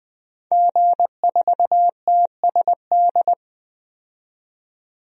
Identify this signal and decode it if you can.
Morse code, heard as a keyed tone, about 20 wpm, 700 Hz. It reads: G4TSD
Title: DXCC-100 Call Signs - 5 Characters - 20wpm